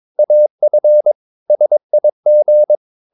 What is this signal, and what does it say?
Morse code, heard as a keyed tone, about 22 wpm, 600 Hz. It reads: AF SIG
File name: Ferie QRS 22wpm